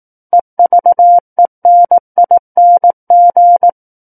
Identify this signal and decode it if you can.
Morse code, heard as a keyed tone, about 18 wpm, 700 Hz. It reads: EVENING
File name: 18 WPM